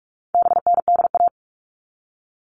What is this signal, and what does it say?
Morse code, heard as a keyed tone, about 44 wpm, 700 Hz. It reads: BNDA
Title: All Cut Numbers - 4-Digit 45wpm